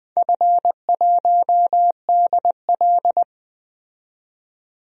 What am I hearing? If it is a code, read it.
Morse code, heard as a keyed tone, about 21 wpm, 700 Hz. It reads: F1DL
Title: DXCC-100 Call Signs - 4 Characters - 20wpm